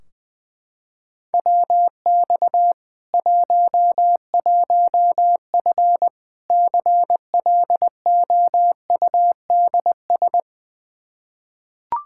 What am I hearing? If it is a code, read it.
Morse code, heard as a keyed tone, about 19 wpm, 700 Hz. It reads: WX 11F CLOUDS
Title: Practice QSO - Exchange 2 - Info - 20wpm